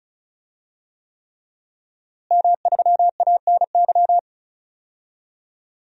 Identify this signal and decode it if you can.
Morse code, heard as a keyed tone, about 34 wpm, 700 Hz. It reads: M3ANY
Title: DXCC-100 Call Signs - 5 Characters - Repeated 3x - 35wpm